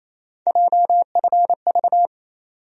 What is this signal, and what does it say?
Morse code, heard as a keyed tone, about 28 wpm, 700 Hz. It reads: JFV
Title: USA Call Sign Postfixes - 3 Letters 28wpm